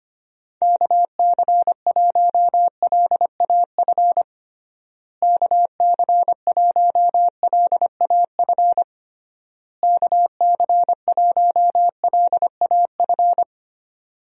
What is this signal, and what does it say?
Morse code, heard as a keyed tone, about 25 wpm, 700 Hz. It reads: KC1LAF KC1LAF KC1LAF
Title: USA Call Signs - 6 Characters - Repeated 3x - 25wpm